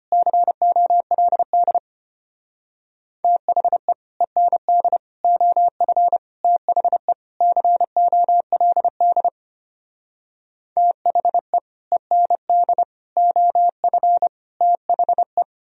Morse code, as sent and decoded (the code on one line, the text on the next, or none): -.-. --- .-.. -.. / - .... / . -. -.. / --- ..-. / - .... / -.-. --- .-.. -.. / - .... / . -. -.. / --- ..-. / - .... .
COLD THE END OF THE COLD THE END OF THE